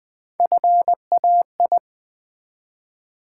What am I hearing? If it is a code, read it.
Morse code, heard as a keyed tone, about 20 wpm, 700 Hz. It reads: FAI
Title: Build A Word - 1 Letter At A Time - 4-Letter Words - 20wpm